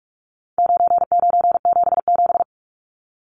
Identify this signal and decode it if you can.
Morse code, heard as a keyed tone, about 44 wpm, 700 Hz. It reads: 9977
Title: Numbers - 4-Digit 45wpm